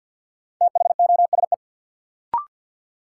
Morse code, heard as a keyed tone, about 53 wpm, 700 Hz. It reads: THOSE
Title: Sets of 2 Words - New Word Drill - 50wpm